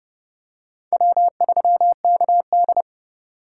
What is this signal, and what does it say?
Morse code, heard as a keyed tone, about 30 wpm, 700 Hz. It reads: W3KD
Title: USA Call Signs - 4 Characters - 4x Character Spacing - 30wpm